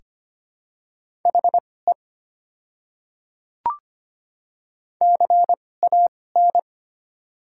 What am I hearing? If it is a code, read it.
Morse code, heard as a keyed tone, about 25 wpm, spaced as for 18 wpm, 700 Hz. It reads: HE CAN
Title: 2x Character Spacing - Top 200 Words - 25wpm